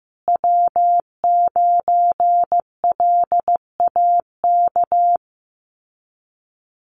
Morse code, heard as a keyed tone, about 15 wpm, 700 Hz. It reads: W9LAK
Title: USA Call Signs Repeated 3x 15wpm